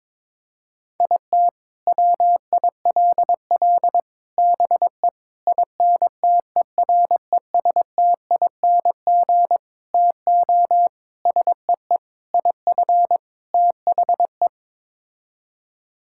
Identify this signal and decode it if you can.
Morse code, heard as a keyed tone, about 22 wpm, 700 Hz. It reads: IT WILL BE INTERESTING TO SEE IF THE
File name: practice / 22wpm / Sets of 8 Words - 22wpm